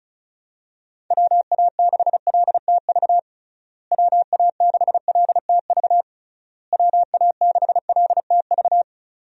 Morse code, heard as a keyed tone, about 36 wpm, 700 Hz. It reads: WA6LTV WA6LTV WA6LTV
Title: USA Call Signs - 6 Characters - Repeated 3x - 35wpm